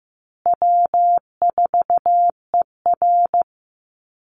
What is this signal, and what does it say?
Morse code, heard as a keyed tone, about 15 wpm, 700 Hz. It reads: W4ER
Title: Call Signs Repeated 3x - 15wpm